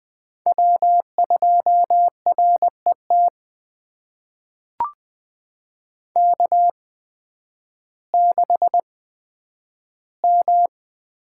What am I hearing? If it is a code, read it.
Morse code, heard as a keyed tone, about 20 wpm, spaced as for 11 wpm, 700 Hz. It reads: W2RET K6M
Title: USA Call Signs - 8x Character Spacing - 20wpm